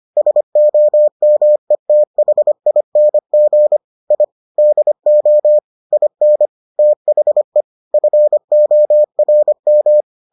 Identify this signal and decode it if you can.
Morse code, heard as a keyed tone, about 25 wpm, 600 Hz. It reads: SOMETHING I DO IN THE FORM